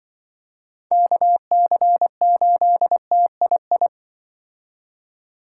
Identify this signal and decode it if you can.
Morse code, heard as a keyed tone, about 25 wpm, 700 Hz. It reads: KC8TII
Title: Speed Racing - USA 2x3 Call Signs 20wpm